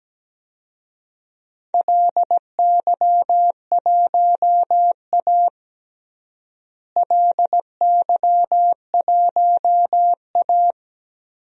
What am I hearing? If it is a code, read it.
Morse code, heard as a keyed tone, about 17 wpm, 700 Hz. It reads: LY1A LY1A
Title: DXCC-100 Call Signs - 4 Characters - Repeated 3x - 17wpm